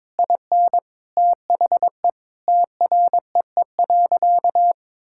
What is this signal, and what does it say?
Morse code, heard as a keyed tone, about 22 wpm, 700 Hz. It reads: IN THE TREE.
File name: practice / 22wpm / Sentences with 2-Word Phrases - 22wpm